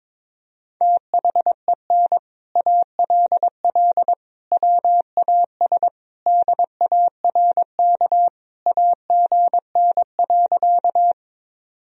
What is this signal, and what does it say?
Morse code, heard as a keyed tone, about 22 wpm, 700 Hz. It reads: THEN ALL WAS DARK AGN.
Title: Sentences from Top 500 Words and Abbreviations 22wpm